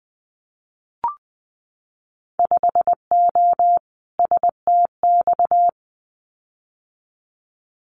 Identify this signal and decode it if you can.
Morse code, heard as a keyed tone, about 20 wpm, 700 Hz. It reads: 5O STX
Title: Winter Field Day Exchanges 20wpm